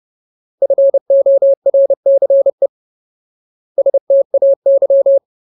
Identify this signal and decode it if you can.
Morse code, heard as a keyed tone, about 30 wpm, 550 Hz. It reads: FORCE STAY
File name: Ultimate Word List / 30WPM Ultimate